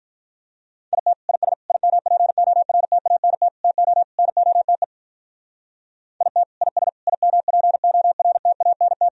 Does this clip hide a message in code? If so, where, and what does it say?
Yes: Morse code from 0.9 to 9.1 s, heard as a keyed tone, about 44 wpm, 700 Hz. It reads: IT IS IMPORTANT TO NOTE IT IS IMPORTANT